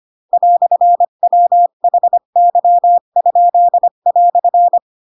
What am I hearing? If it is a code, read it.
Morse code, heard as a keyed tone, about 25 wpm, 700 Hz. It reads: "WHY?"
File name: Pinocchio / Pinocchio ch4 20 25 ik1hge